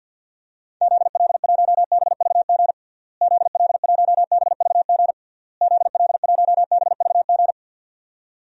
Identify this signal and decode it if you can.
Morse code, heard as a keyed tone, about 51 wpm, 700 Hz. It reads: ZL1BVG ZL1BVG ZL1BVG
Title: DXCC-100 Call Signs - 6 Characters - Repeated 3x - 50wpm